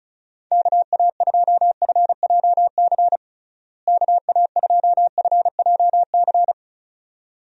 Morse code, heard as a keyed tone, about 34 wpm, 700 Hz. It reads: KA2FJC KA2FJC
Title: USA Call Signs - 6 Characters - Repeated 2x - 35wpm